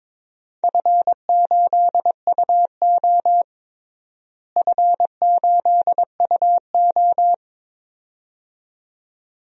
Morse code, heard as a keyed tone, about 22 wpm, 700 Hz. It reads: F8UO F8UO
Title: DXCC-100 Call Signs - 4 Characters - Repeated 2x - 22wpm